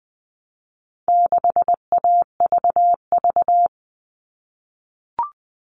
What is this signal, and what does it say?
Morse code, heard as a keyed tone, about 20 wpm, 700 Hz. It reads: 6AVV
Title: All Cut Numbers - 4-Digit 20wpm